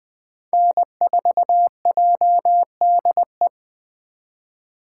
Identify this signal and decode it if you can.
Morse code, heard as a keyed tone, about 20 wpm, 700 Hz. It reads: N4JDE
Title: USA Call Signs - 5 Characters - 2x Character Spacing - 20wpm